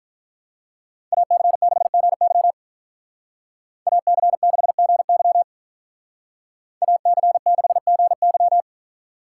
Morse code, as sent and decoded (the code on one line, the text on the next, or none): .- -.-. -.... --. -.-- / .- -.-. -.... --. -.-- / .- -.-. -.... --. -.--
AC6GY AC6GY AC6GY